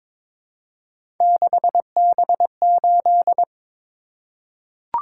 Morse code, heard as a keyed tone, about 22 wpm, 700 Hz. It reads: 6B8
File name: practice / 22wpm / Three Letter-Number Combinations - 22wpm